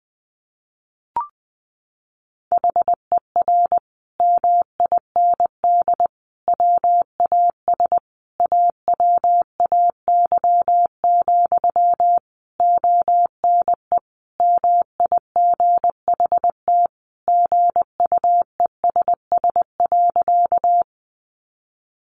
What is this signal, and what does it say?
Morse code, heard as a keyed tone, about 20 wpm, 700 Hz. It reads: HER MIND WAS AWAY, ONE MIGHT GUESS.
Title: Sentences from Top 1500 Words 20wpm